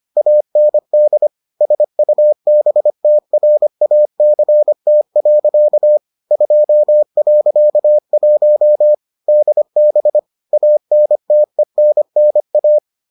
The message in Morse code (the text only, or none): AND SUBTRACT. 2.1 DB ANTENNA